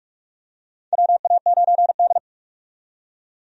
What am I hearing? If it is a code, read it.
Morse code, heard as a keyed tone, about 46 wpm, 700 Hz. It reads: WA9D
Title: USA Call Signs - 4 Characters - 45wpm